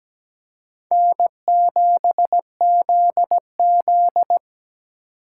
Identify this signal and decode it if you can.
Morse code, heard as a keyed tone, about 17 wpm, 700 Hz. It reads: N7ZZ
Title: USA Call Signs - 4 Characters - 8x Character Spacing - 17wpm